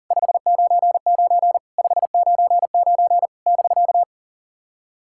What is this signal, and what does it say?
Morse code, heard as a keyed tone, about 44 wpm, 700 Hz. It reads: 599 599 <BK>
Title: SOTA QSO Practice 40wpm